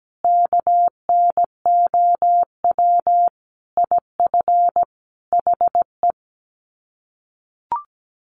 Morse code, heard as a keyed tone, about 17 wpm, 700 Hz. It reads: KNOW IF HE
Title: Sets of 5 Words Encore2 17wpm